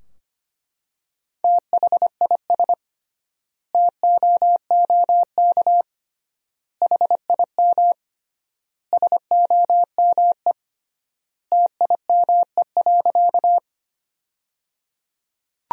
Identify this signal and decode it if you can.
Morse code, heard as a keyed tone, about 25 wpm, 700 Hz. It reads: THIS TOOK HIM SOME TIME.
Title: 3x Word-Spacing - Sentences from Top 300 Words - 25wpm